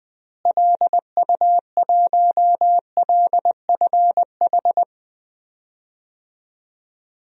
Morse code, heard as a keyed tone, about 20 wpm, 700 Hz. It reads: LU1LFH
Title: DXCC-100 Call Signs - 6 Characters - Repeated 3x - 20wpm